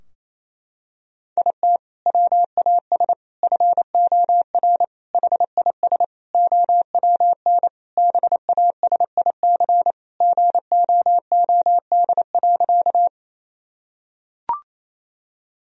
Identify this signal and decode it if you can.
Morse code, heard as a keyed tone, about 28 wpm, 700 Hz. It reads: IT WAS FOR HIS OWN BASIC GOOD.
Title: Sentences from Top 1000 Words - Review Part 3 - 28wpm